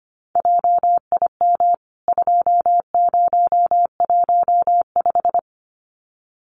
Morse code, heard as a keyed tone, about 25 wpm, 690 Hz. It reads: JIM 2015
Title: CWT Exchange 25wpm